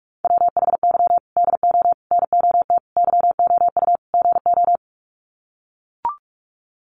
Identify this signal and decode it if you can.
Morse code, heard as a keyed tone, about 44 wpm, 700 Hz. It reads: WHY DO NOT YOU GO
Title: Sets of 5 Words Encore 45wpm